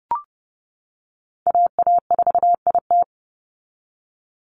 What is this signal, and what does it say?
Morse code, heard as a keyed tone, about 30 wpm, 700 Hz. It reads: AA4IT
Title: NA SOTA Chaser Callsigns 30wpm